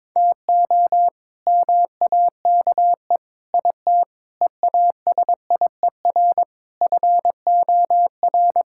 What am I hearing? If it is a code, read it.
Morse code, heard as a keyed tone, about 22 wpm, 700 Hz. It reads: TO MAKE IT EASIER FOR
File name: practice / 22wpm / Sets of 7 Words - 22wpm